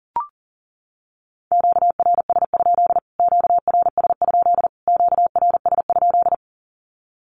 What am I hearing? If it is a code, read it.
Morse code, heard as a keyed tone, about 40 wpm, 700 Hz. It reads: QRS? QRS? QRS?
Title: Q-Codes Repeated 3x 40wpm